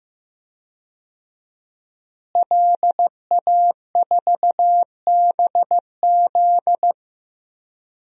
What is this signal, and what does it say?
Morse code, heard as a keyed tone, about 16 wpm, 700 Hz. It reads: LA4BZ
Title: DXCC-100 Call Signs - 5 Characters - Repeated 2x - 15wpm